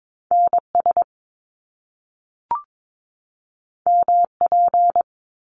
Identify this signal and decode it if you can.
Morse code, heard as a keyed tone, about 22 wpm, 700 Hz. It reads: NS MP
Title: Top Two Letter Combinations in Words 22wpm